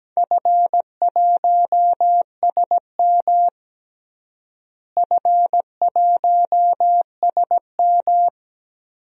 Morse code, heard as a keyed tone, about 17 wpm, 700 Hz. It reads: F1SM F1SM
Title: DXCC-100 Call Signs - 4 Characters - Repeated 3x - 17wpm